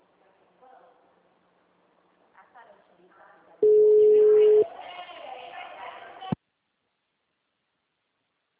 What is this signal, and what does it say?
Ringback tone (Europe, 425 Hz), with speech.